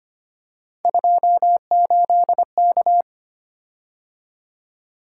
Morse code, heard as a keyed tone, about 25 wpm, 700 Hz. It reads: 28K
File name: Three Letter-Number Combinations - 25wpm